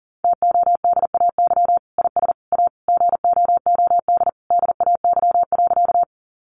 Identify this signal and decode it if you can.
Morse code, heard as a keyed tone, about 40 wpm, 700 Hz. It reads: TODAY IS A GOOD DAY.
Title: Sentences from Top 600 Words 40wpm